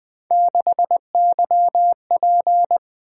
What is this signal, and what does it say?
Morse code, heard as a keyed tone, about 20 wpm, 700 Hz. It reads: 6YP
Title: Three Letter-Number Combinations - 20wpm